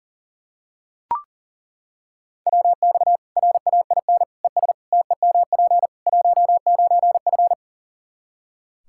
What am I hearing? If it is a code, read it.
Morse code, heard as a keyed tone, about 40 wpm, 700 Hz. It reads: WX RAIN ES TEMP 19F
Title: Practice QSO - Exchange 2 - Info - 40wpm